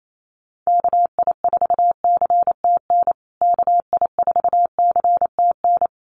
Morse code, heard as a keyed tone, about 28 wpm, 700 Hz. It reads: KI4CTN KI4CTN
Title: USA Call Signs Repeated 2x 28wpm